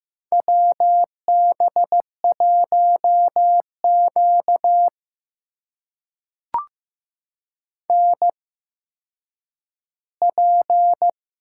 Morse code, heard as a keyed tone, about 15 wpm, spaced as for 9 wpm, 700 Hz. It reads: WB1Q NP
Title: USA Call Signs - 4 Characters - 8x Character Spacing - 15wpm